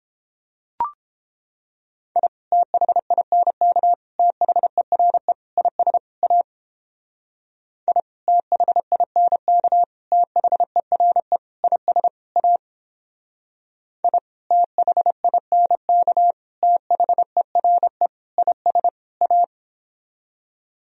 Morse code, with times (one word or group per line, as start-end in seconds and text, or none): 2.2-2.3 I
2.5-3.9 THINK
4.2-5.3 THERE
5.6-6.0 IS
6.2-6.4 A
7.9-8.0 I
8.3-9.8 THINK
10.1-11.4 THERE
11.6-12.1 IS
12.4-12.6 A
14.0-14.2 I
14.5-16.3 THINK
16.6-18.1 THERE
18.4-18.9 IS
19.2-19.4 A